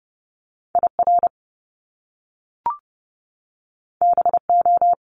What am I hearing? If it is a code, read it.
Morse code, heard as a keyed tone, about 30 wpm, 700 Hz. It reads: IR BO
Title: Top Two Letter Combinations in Words 30wpm